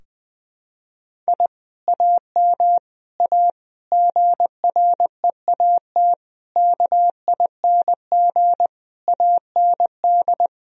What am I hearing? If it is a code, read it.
Morse code, heard as a keyed tone, about 20 wpm, 700 Hz. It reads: I AM A GREAT KING AND
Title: Sentences from Top 1000 Words - Review Part 4 - 20wpm